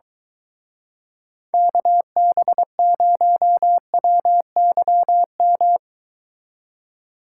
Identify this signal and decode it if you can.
Morse code, heard as a keyed tone, about 22 wpm, 700 Hz. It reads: KB0WYM